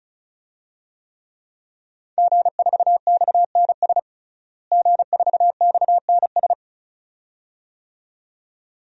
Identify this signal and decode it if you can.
Morse code, heard as a keyed tone, about 34 wpm, 700 Hz. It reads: G4XNS G4XNS
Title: DXCC-100 Call Signs - 5 Characters - Repeated 2x - 35wpm